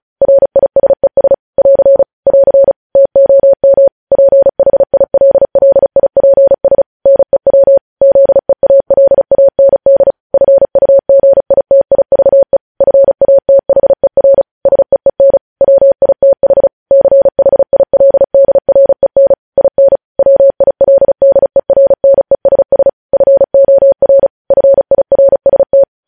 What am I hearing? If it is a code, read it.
Morse code, heard as a keyed tone, about 36 wpm, 550 Hz. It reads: RISES <AR> <AR> TOM PHILLIPS NEW ZEALAND FUGITIVE FATHER SEEN WITH CHILDREN IN WILDERNESS FOR FIRST